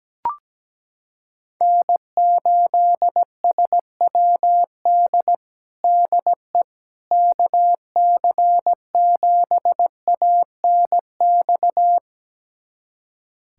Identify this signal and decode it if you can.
Morse code, heard as a keyed tone, about 17 wpm, 700 Hz. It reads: N8SWD DE KC7ANX
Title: USA Callsign DE Callsign 17wpm